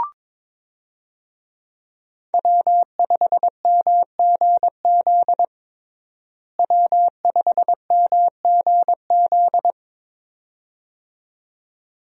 Morse code, heard as a keyed tone, about 22 wpm, 700 Hz. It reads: W5MGZ W5MGZ
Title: USA Call Signs - 5 Characters - Repeated 2x - 22wpm